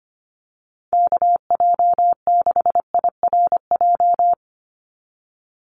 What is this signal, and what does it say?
Morse code, heard as a keyed tone, about 25 wpm, 700 Hz. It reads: KJ6IRJ
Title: USA Call Signs 25wpm